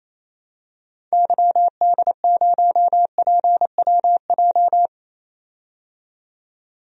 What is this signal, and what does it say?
Morse code, heard as a keyed tone, about 28 wpm, 700 Hz. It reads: YD0PWJ
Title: DXCC-100 Call Signs - 6 Characters - 28wpm